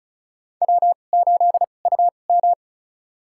Morse code, heard as a keyed tone, about 36 wpm, spaced as for 25 wpm, 700 Hz. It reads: W8UM
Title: USA Call Signs - 2x Character Spacing - 35wpm